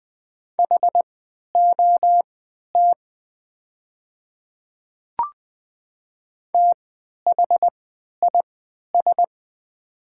Morse code, heard as a keyed tone, about 20 wpm, spaced as for 11 wpm, 700 Hz. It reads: HOT THIS
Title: 3x Character Spacing - Top 100 Words - 20wpm